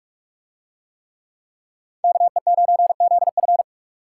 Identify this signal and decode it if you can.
Morse code, heard as a keyed tone, about 44 wpm, 700 Hz. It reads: KE9ZF